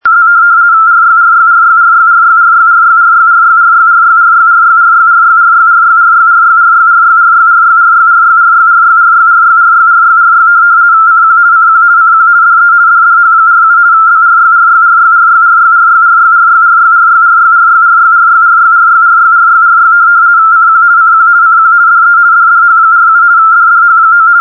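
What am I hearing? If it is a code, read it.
RTTY, about 50 baud. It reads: Z RYRYRYRYRYRYRYRYRYRYRYRYRYRYRYRYRYRYRYRYRYRYRYRYRYRYRYRYRYRYRYRY CQ CQ CQ DE DDH47 BRYRYRYRYRYRYRYRYRYRYRYRYRYRYRYRYRYRYRY CQ CQ CQ DE DDH47 DDH9
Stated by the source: RTTY test tape from Hamburg Meteo (DWD, Deutscher Wetter Dienst) telegraphic weather service on 147.3, recorded from longwave with only 85 Hz shift.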